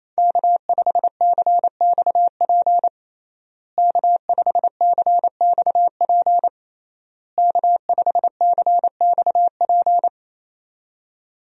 Morse code, heard as a keyed tone, about 27 wpm, 700 Hz. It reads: K5CXP K5CXP K5CXP
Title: USA Call Signs - 5 Characters - Repeated 3x - 28wpm